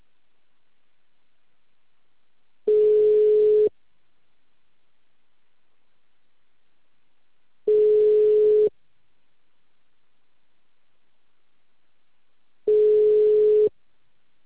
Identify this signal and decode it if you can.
Ringback tone (Europe, 425 Hz).